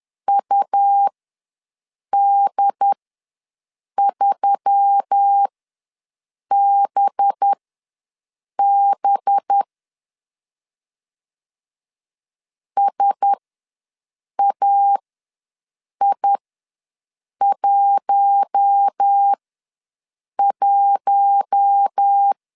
Morse code, heard as a keyed tone, about 10 wpm, spaced as for 6 wpm, 800 Hz. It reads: UD3BB SAI11
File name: LEZIONI / 5 LBVC / 5 LB ws 10wpm 3616